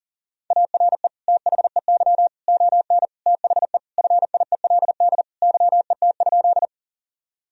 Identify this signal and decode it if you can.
Morse code, heard as a keyed tone, about 40 wpm, 700 Hz. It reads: ARE THEY ON THE FIELD YET?
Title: Sentences from Top 1000 Words - Review Part 2 - 40wpm